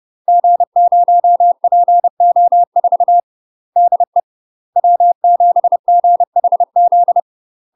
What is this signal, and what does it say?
Morse code, heard as a keyed tone, about 30 wpm, 700 Hz. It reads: G0PO4 DE W7GHZ
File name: UK MASTER QSOs TRAINING / 30WPM / UK MASTER QSOs 0030